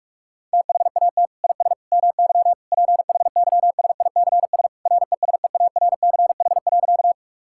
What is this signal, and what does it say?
Morse code, heard as a keyed tone, about 46 wpm, 700 Hz. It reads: THAT IS MY PHYSICS RESEARCH.
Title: Sentences from Top 1000 Words - Review Part 5 - 45wpm